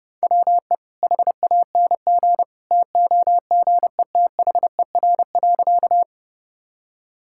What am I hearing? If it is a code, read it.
Morse code, heard as a keyed tone, about 32 wpm, 700 Hz. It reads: WE HANG TOGETHER.
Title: Sentences from Top 1600 Words 30wpm